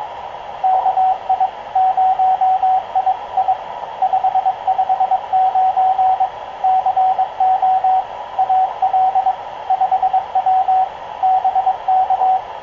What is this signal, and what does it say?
Morse code, heard as a keyed tone, about 22 wpm, 760 Hz, with radio noise. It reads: KI0II 559 CO AL 5W BK
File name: k0frp ki0ii wa8bxn fox utc